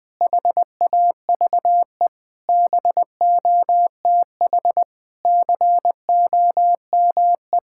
Morse code, heard as a keyed tone, about 20 wpm, 700 Hz. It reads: HAVE BOTH COME
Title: Sentences from Top 1600 Words 20wpm